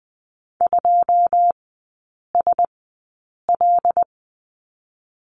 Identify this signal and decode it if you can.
Morse code, heard as a keyed tone, about 20 wpm, spaced as for 8 wpm, 700 Hz. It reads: 2SL
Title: 121 Lesson 20e - Number 3 - Three Characters 20wpm